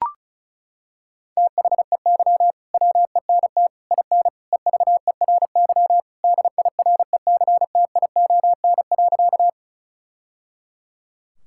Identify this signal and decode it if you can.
Morse code, heard as a keyed tone, about 34 wpm, 700 Hz. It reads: THEY WENT IN EVERY DIRECTION.
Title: Sentences from Top 1000 Words - Review Part 5 - 35wpm